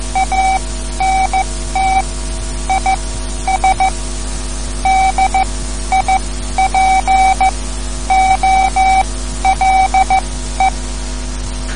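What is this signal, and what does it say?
Morse code, heard as a keyed tone, about 14 wpm, spaced as for 11 wpm, 760 Hz. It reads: ANT IS DIPOLE
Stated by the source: Morse code lesson, third lesson in a series for learning the Morse alphabet.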